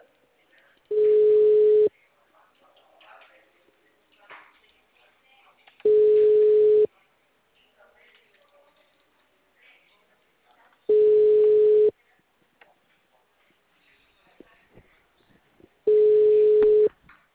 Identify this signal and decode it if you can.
Ringback tone (Europe, 425 Hz), with speech.
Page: Atibaia - SP 1 (CP Cartao)